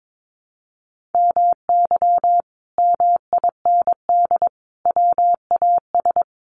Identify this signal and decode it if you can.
Morse code, heard as a keyed tone, about 22 wpm, 700 Hz. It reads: MY MIND WAS
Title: Sentences from Top 700 Words 22wpm